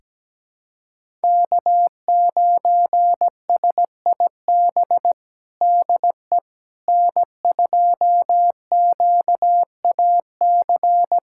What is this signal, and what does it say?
Morse code, heard as a keyed tone, about 17 wpm, 700 Hz. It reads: K9SIB DE N2QAC
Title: USA Callsign DE Callsign - 5 Characters - 17wpm